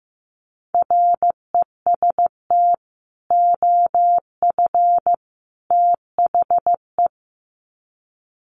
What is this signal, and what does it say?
Morse code, heard as a keyed tone, about 15 wpm, 700 Hz. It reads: REST OF THE
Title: Sets of 3 Words 15wpm